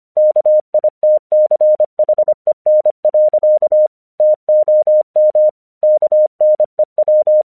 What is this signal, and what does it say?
Morse code, heard as a keyed tone, about 24 wpm, 610 Hz. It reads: KITCHEN. TOM KNEW
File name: 1 English 25wpm / AQC21